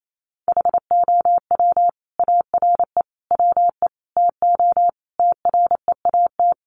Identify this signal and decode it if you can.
Morse code, heard as a keyed tone, about 28 wpm, 700 Hz. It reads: HOW ARE WE TO TREAT
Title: Sentences from Top 1300 Words 28wpm